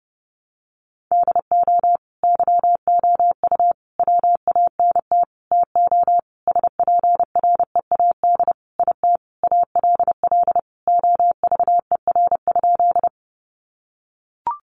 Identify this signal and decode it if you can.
Morse code, heard as a keyed tone, about 30 wpm, 700 Hz. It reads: DO YOU WANT TO SPREAD IT ALL OVER?
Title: Sentences from Top 1200 Words 30wpm